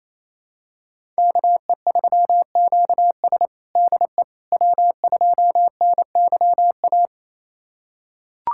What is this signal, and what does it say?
Morse code, heard as a keyed tone, about 28 wpm, 700 Hz. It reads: KE3QS DE W2NYA
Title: USA Callsign DE Callsign - 5 Characters - 28wpm